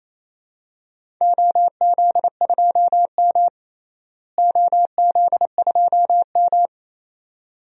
Morse code, heard as a keyed tone, about 28 wpm, 700 Hz. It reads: OZ2M OZ2M
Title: DXCC-100 Call Signs - 4 Characters - Repeated 2x - 28wpm